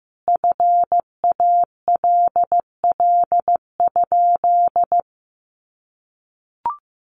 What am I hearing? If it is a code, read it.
Morse code, heard as a keyed tone, about 15 wpm, 700 Hz. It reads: FALL?
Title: Sentences from Top 500 Words 15wpm